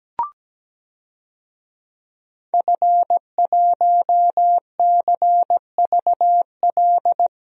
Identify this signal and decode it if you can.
Morse code, heard as a keyed tone, about 17 wpm, 700 Hz. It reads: F1CVL
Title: DXCC-100 Call Signs - 5 Characters - 17wpm